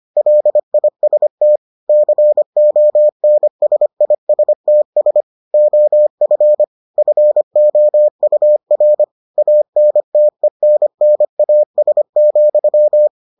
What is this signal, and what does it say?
Morse code, heard as a keyed tone, about 25 wpm, 600 Hz. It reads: LIST CONSISTS OF FOUR ANTENNAS,